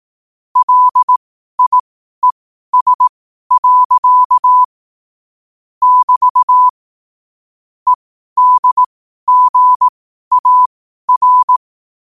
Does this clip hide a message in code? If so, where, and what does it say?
Yes: Morse code from 0.6 to 11.6 s, heard as a keyed tone, about 18 wpm, spaced as for 12 wpm, 1000 Hz. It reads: LIES. <BT> EDGAR